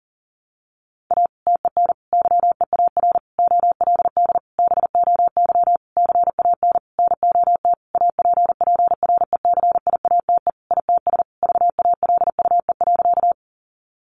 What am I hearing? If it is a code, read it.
Morse code, heard as a keyed tone, about 40 wpm, 700 Hz. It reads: A TEN YEAR OLD BOY CAN NOT APPRECIATE ITS VALUE.
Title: Sentences from Top 1300 Words 40wpm